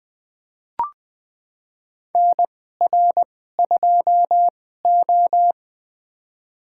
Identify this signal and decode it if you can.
Morse code, heard as a keyed tone, about 20 wpm, spaced as for 14 wpm, 700 Hz. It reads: NR2O
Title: USA Call Signs - 4 Characters - 2x Character Spacing - 20wpm